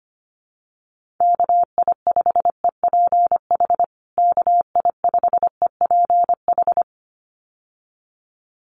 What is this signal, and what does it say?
Morse code, heard as a keyed tone, about 25 wpm, 700 Hz. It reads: KI5EPH KI5EPH
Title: USA Call Signs Repeated 2x 25wpm